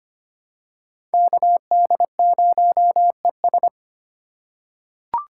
Morse code, heard as a keyed tone, about 25 wpm, 700 Hz. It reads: KD0ES
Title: USA Call Signs - 4x Character Spacing - 25wpm